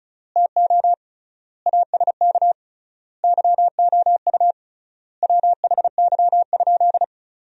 Morse code, heard as a keyed tone, about 34 wpm, 700 Hz. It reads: TO ASK YOU WHY?
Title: 3x Word-Spacing - Sentences from Top 200 Words - 35wpm